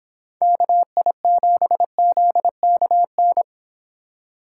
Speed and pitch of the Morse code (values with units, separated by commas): 25 wpm, 700 Hz